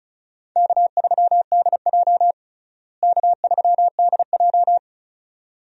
Morse code, heard as a keyed tone, about 36 wpm, 700 Hz. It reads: K3DJ K3DJ
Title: USA Call Signs - 4 Characters - Repeated 2x - 35wpm